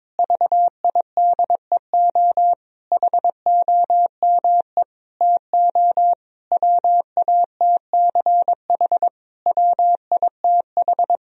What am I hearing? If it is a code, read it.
Morse code, heard as a keyed tone, about 22 wpm, 700 Hz. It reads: VIDEO HOME TO WATCH WITH
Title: Sentences from Top 1000 Words - Review Part 4 - 22wpm